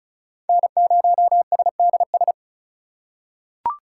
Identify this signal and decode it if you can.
Morse code, heard as a keyed tone, about 36 wpm, 700 Hz. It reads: N0SDS
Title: USA Call Signs - 3x Character Spacing - 35wpm